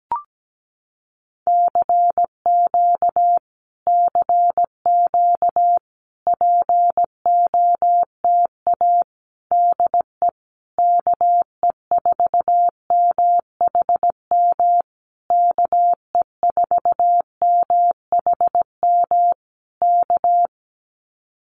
Morse code, heard as a keyed tone, about 17 wpm, 700 Hz. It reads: CQ CQ POTA DE KE4MHM KE4MHM K
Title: Common QSO Phrases Encore 17wpm